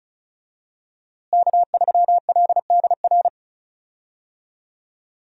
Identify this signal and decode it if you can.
Morse code, heard as a keyed tone, about 36 wpm, 700 Hz. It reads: K3LDR
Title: USA Call Signs - 5 Characters - Repeated 3x - 35wpm